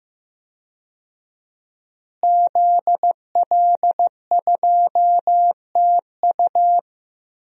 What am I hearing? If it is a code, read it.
Morse code, heard as a keyed tone, about 15 wpm, 700 Hz. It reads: ZL2TU
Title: DXCC-100 Call Signs - 5 Characters - Repeated 3x - 15wpm